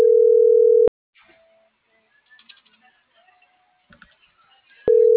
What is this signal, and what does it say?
Ringback tone (North America), 440 + 480 Hz.